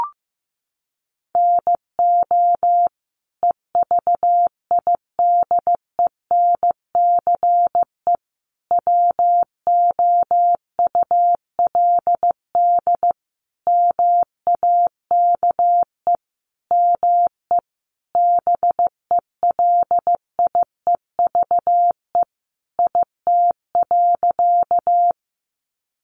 Morse code, heard as a keyed tone, about 15 wpm, 700 Hz. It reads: NO EVIDENCE WOULD MAKE ME BELIEVE IT.
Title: Sentences from Top 1300 Words 15wpm